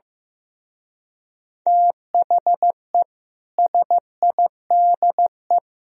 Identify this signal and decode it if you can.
Morse code, heard as a keyed tone, about 15 wpm, 700 Hz. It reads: THE SIDE
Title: Speed-Racing - Sets of 5 Words 15wpm